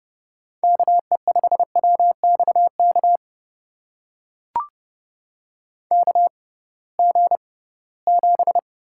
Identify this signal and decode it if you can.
Morse code, heard as a keyed tone, about 32 wpm, spaced as for 20 wpm, 700 Hz. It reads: KE5WXK KG7
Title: USA Call Signs - 6 Characters - 6x Character Spacing - 30wpm